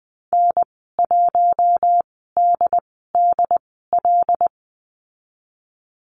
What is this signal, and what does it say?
Morse code, heard as a keyed tone, about 20 wpm, spaced as for 15 wpm, 700 Hz. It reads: N1DDL
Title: 088 Lesson 15g - Number 5 - Call Signs 20wpm